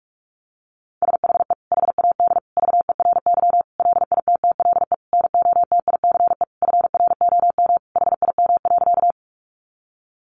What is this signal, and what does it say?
Morse code, heard as a keyed tone, about 44 wpm, 700 Hz. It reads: SHE HAD VERY LITTLE NOTICE FROM HIM.
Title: Sentences from Top 400 Words 45wpm